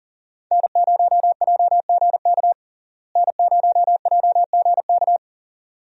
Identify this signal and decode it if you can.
Morse code, heard as a keyed tone, about 40 wpm, 700 Hz. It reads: N0JGK N0JGK
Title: USA Call Signs - 5 Characters - Repeated 2x - 40wpm